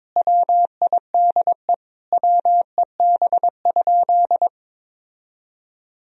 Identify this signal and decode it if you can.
Morse code, heard as a keyed tone, about 22 wpm, 700 Hz. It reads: WIDE WEB?
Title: Sentences from Top 1000 Words - Review Part 4 - 22wpm